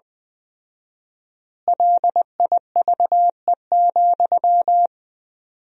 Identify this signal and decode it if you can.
Morse code, heard as a keyed tone, about 20 wpm, 700 Hz. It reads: LIVE,
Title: 4x Word-Spacing - Sentences from Top 200 Words - 20wpm